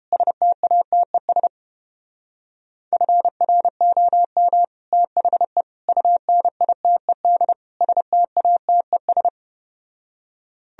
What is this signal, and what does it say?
Morse code, heard as a keyed tone, about 32 wpm, 700 Hz. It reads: STATES FROM THE UNITED STATES